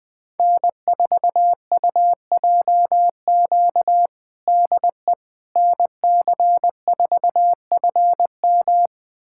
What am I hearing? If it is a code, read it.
Morse code, heard as a keyed tone, about 21 wpm, 700 Hz. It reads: N4UJQ DE NC4FM
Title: USA Callsign DE Callsign - 5 Characters - 20wpm